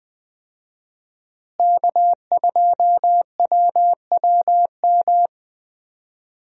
Morse code, heard as a keyed tone, about 20 wpm, 700 Hz. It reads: K2WWM
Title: USA Call Signs - 5 Characters - Repeated 2x - 20wpm